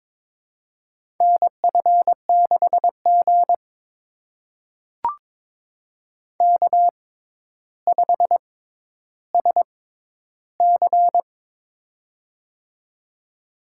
Morse code, heard as a keyed tone, about 22 wpm, spaced as for 11 wpm, 700 Hz. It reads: NF6G K5SC
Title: USA Call Signs - 4 Characters - 6x Character Spacing - 22wpm